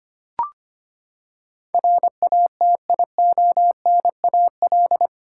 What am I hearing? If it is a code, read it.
Morse code, heard as a keyed tone, about 25 wpm, 700 Hz. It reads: RATIONAL
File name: Eight Letter Words - 25wpm